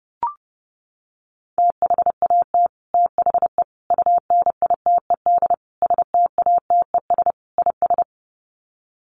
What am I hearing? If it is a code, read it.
Morse code, heard as a keyed tone, about 30 wpm, 700 Hz. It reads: THAT THE UNITED STATES IS
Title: Sets of 5 Words Encore 30wpm